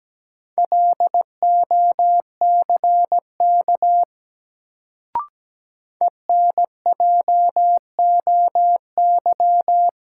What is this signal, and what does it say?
Morse code, heard as a keyed tone, about 17 wpm, 700 Hz. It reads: LOCK ENJOY
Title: Sentences from Top 1300 Words - New Word Drill Alt - 17wpm